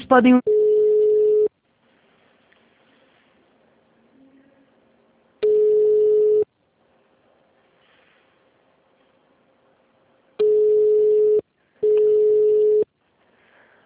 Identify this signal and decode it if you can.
Ringback tone (Europe, 425 Hz), with speech.
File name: out- -2003- - - .45